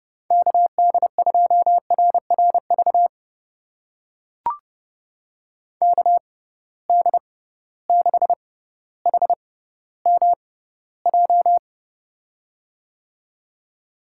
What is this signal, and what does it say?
Morse code, heard as a keyed tone, about 30 wpm, spaced as for 15 wpm, 700 Hz. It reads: KD2RRV KD6HMJ
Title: USA Call Signs - 6 Characters - 6x Character Spacing - 30wpm